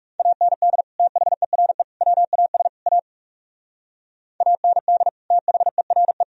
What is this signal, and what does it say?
Morse code, heard as a keyed tone, about 35 wpm, 700 Hz. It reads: AND THERE WAS A AND THERE
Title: Speed-Racing - Sets of 4 Words 30wpm